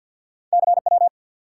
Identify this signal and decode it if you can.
Morse code, heard as a keyed tone, about 53 wpm, 700 Hz. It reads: CW
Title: Top 180 QSO Elements 50wpm